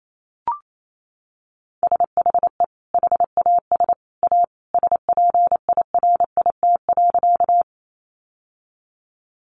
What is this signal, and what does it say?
Morse code, heard as a keyed tone, about 28 wpm, 700 Hz. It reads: SHE HAS A SPIRIT.
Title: Sentences from Top 1400 Words 28wpm